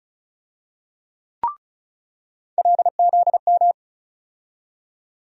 Morse code, heard as a keyed tone, about 34 wpm, 700 Hz. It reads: LZM
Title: USA Call Sign Postfixes - 3 Letters 35wpm